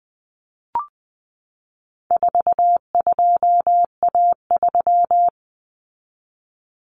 Morse code, heard as a keyed tone, about 20 wpm, 700 Hz. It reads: 42A3
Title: Common Cut Numbers - 4-Digit 20wpm